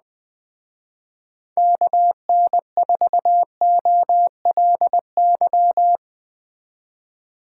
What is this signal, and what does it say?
Morse code, heard as a keyed tone, about 20 wpm, 700 Hz. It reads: KN4OLY
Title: Speed Racing - USA 2x3 Call Signs 17wpm